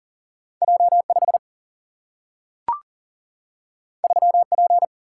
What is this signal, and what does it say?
Morse code, heard as a keyed tone, about 42 wpm, 700 Hz. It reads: J5 3P
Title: Single-Letter Single-Number 40wpm